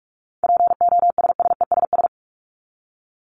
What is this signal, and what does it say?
Morse code, heard as a keyed tone, about 44 wpm, 700 Hz. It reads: POSSESS
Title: Top 1751-2000 Words 45wpm